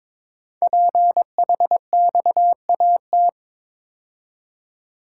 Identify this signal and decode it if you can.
Morse code, heard as a keyed tone, about 22 wpm, 700 Hz. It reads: PHXAT
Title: Code Groups - 5 Letter:Numbers - 22wpm